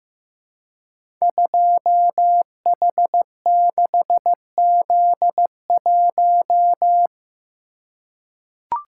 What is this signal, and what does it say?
Morse code, heard as a keyed tone, about 15 wpm, 700 Hz. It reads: 2H6Z1
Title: Code Groups - 5 Letter:Numbers - 15wpm